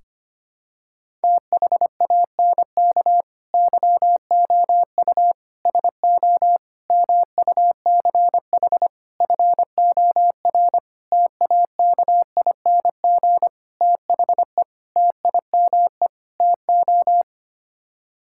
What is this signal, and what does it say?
Morse code, heard as a keyed tone, about 25 wpm, 700 Hz. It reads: THANK YOU SO MUCH FOR TAKING THE TIME TO
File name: Sets of 9 Words - 25wpm